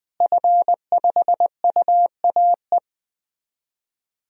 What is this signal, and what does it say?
Morse code, heard as a keyed tone, about 20 wpm, 700 Hz. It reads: F5UAE
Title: DXCC-100 Call Signs - 5 Characters - Repeated 3x - 20wpm